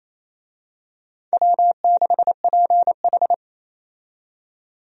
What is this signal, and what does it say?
Morse code, heard as a keyed tone, about 28 wpm, 700 Hz. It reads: W6PH
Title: USA Call Signs - 4 Characters - 28wpm